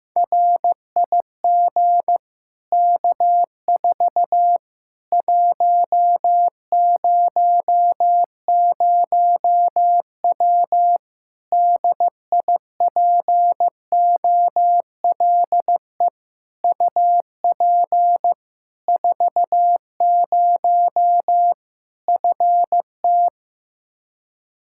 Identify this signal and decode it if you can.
Morse code, heard as a keyed tone, about 15 wpm, 700 Hz. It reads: RIG K4 100W DIPOLE UP 40 FT
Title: Practice QSO - Exchange 2 - Info - 15wpm